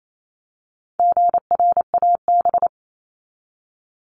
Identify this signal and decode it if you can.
Morse code, heard as a keyed tone, about 28 wpm, 700 Hz. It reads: GRAB
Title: Top 1751-2000 Words 28wpm